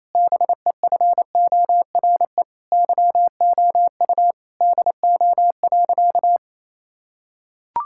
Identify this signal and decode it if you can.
Morse code, heard as a keyed tone, about 28 wpm, 700 Hz. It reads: BEFORE YOU DO.
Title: Sentences from Top 1000 Words - Review Part 6 - 28wpm